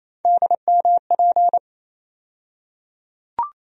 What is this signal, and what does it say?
Morse code, heard as a keyed tone, about 28 wpm, 700 Hz. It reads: DMP
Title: USA Call Sign Postfixes - 3 Letters 28wpm